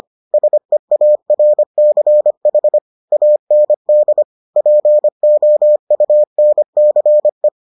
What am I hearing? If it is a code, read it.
Morse code, heard as a keyed tone, about 25 wpm, 600 Hz. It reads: SEARCH AND POUNCE